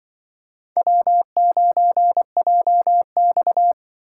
Morse code, heard as a keyed tone, about 25 wpm, 700 Hz. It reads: W9JX
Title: Speed Racing - USA 1x2 Call Signs 20wpm